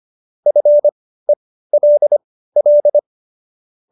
Morse code, heard as a keyed tone, about 25 wpm, spaced as for 15 wpm, 590 Hz. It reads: FELL